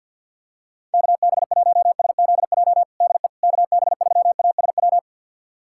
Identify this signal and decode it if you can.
Morse code, heard as a keyed tone, about 47 wpm, 700 Hz. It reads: KB1SZJ DE KB3ASW
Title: USA Callsign DE Callsign - 6 Characters - 50wpm